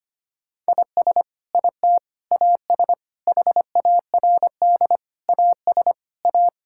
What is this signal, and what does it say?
Morse code, heard as a keyed tone, about 25 wpm, 700 Hz. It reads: IS IT AS HARD AS A
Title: Sentences from Top 1600 Words 25wpm